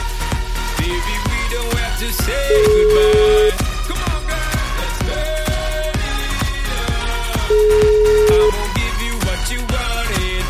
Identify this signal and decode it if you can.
Ringback tone (Europe, 425 Hz), with noise.